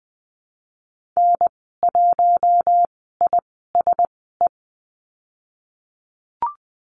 Morse code, heard as a keyed tone, about 20 wpm, spaced as for 15 wpm, 700 Hz. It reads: N1ISE
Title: 088 Lesson 15g - Number 5 - Call Signs 20wpm